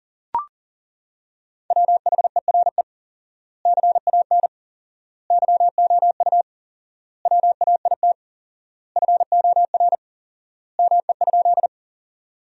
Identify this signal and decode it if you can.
Morse code, heard as a keyed tone, about 40 wpm, 700 Hz. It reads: WHERE CAN YOU WAIT FOR ME?
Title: 4x Word-Spacing - Sentences from Top 500 Words - 40wpm